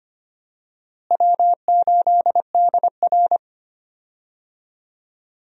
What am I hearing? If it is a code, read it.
Morse code, heard as a keyed tone, about 25 wpm, 700 Hz. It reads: W8DR
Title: USA Call Signs - 4 Characters - Repeated 2x - 25wpm